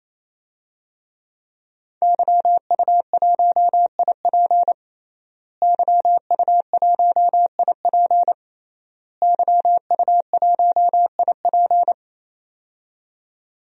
Morse code, heard as a keyed tone, about 28 wpm, 700 Hz. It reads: YU1IP YU1IP YU1IP
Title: DXCC-100 Call Signs - 5 Characters - Repeated 3x - 28wpm